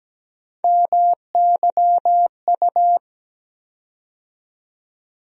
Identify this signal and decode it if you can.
Morse code, heard as a keyed tone, about 17 wpm, 700 Hz. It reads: MYU
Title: USA Call Sign Postfixes - 3 Letters 17wpm